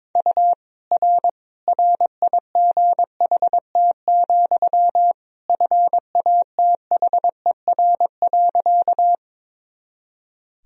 Morse code, heard as a keyed tone, about 22 wpm, 700 Hz. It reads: U R RIGHT, FATHER.
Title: Sentences from Top 300 Words and Abbreviations 22wpm